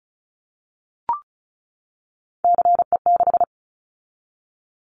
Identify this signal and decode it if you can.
Morse code, heard as a keyed tone, about 34 wpm, 700 Hz. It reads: CE6
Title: Callsign Prefixes for Top 100 DXCC - 35wpm